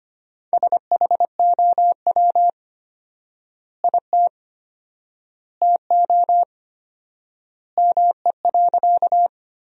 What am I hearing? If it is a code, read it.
Morse code, heard as a keyed tone, about 25 wpm, 700 Hz. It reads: SHOW IT TO ME.
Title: 4x Word-Spacing - Sentences from Top 200 Words - 25wpm